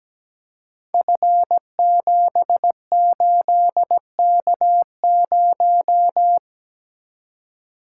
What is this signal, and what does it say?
Morse code, heard as a keyed tone, about 17 wpm, 700 Hz. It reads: F78K0
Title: Code Groups - 5 Letter:Numbers - 17wpm